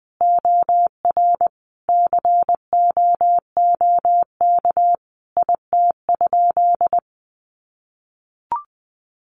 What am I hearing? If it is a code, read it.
Morse code, heard as a keyed tone, about 20 wpm, 700 Hz. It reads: OR COOK IT?
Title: Sentences from Top 1500 Words 20wpm